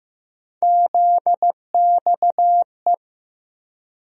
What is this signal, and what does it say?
Morse code, heard as a keyed tone, about 15 wpm, 700 Hz. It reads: ZXE
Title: USA Call Sign Postfixes - 3 Letters 15wpm